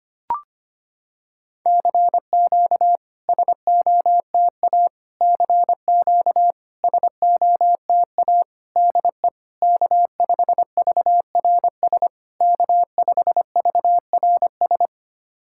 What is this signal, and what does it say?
Morse code, heard as a keyed tone, about 25 wpm, 690 Hz. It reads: CQ SOTA CQ SOTA DE K5VRS K5VRS
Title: SOTA QSO Practice 25wpm